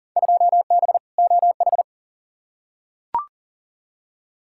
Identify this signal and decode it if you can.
Morse code, heard as a keyed tone, about 40 wpm, 700 Hz. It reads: 2B OH
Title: Field Day Exchanges Encore - 40wpm